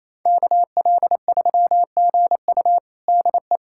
Morse code, heard as a keyed tone, about 28 wpm, 700 Hz. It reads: KL3GU DE
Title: USA Callsign DE Callsign - 5 Characters - 28wpm